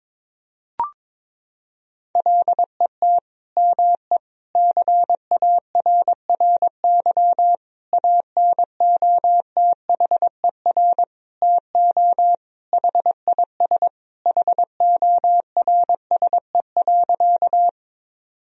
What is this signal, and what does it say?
Morse code, heard as a keyed tone, about 22 wpm, 700 Hz. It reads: LET ME CARRY ANOTHER TO HIS HORSE.
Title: Sentences from Top 1000 Words - Review Part 1 - 22wpm